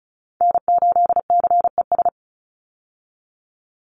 Morse code, heard as a keyed tone, about 36 wpm, 700 Hz. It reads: N8CES
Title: USA Call Signs 35wpm